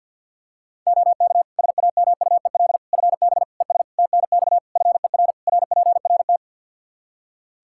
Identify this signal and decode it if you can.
Morse code, heard as a keyed tone, about 46 wpm, 700 Hz. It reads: OK SAMUEL FB ES TNX FER RPRT
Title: K1USN SST QSO Practice 50wpm